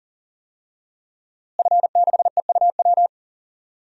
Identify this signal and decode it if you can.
Morse code, heard as a keyed tone, about 40 wpm, 700 Hz. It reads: F6EUW
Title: DXCC-100 Call Signs - 5 Characters - Repeated 2x - 40wpm